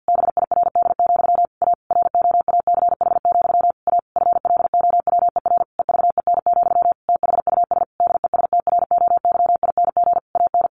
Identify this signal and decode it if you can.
Morse code, heard as a keyed tone, about 50 wpm, 700 Hz. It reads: BIRD, A ROACH, A FLOWER EVEN, THUS DESTROYING AN
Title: A Sound Of Thunder 50wpm